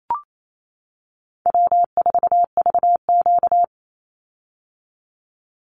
Morse code, heard as a keyed tone, about 28 wpm, 700 Hz. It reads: W4VQ
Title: Easy Call Signs - 28wpm